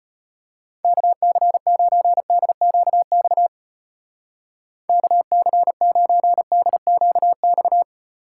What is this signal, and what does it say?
Morse code, heard as a keyed tone, about 31 wpm, 700 Hz. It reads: KC9DQX KC9DQX
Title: Speed Racing - USA 2x3 Call Signs 25wpm